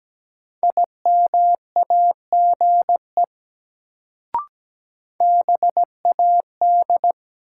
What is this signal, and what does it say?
Morse code, heard as a keyed tone, about 17 wpm, 700 Hz. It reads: IMAGE BAD
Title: Sentences from Top 700 Words - New Word Drill - 17wpm